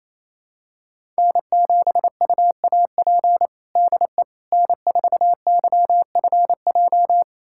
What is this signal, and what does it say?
Morse code, heard as a keyed tone, about 28 wpm, 700 Hz. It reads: N7UAP DE N4YFJ
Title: USA Callsign DE Callsign 28wpm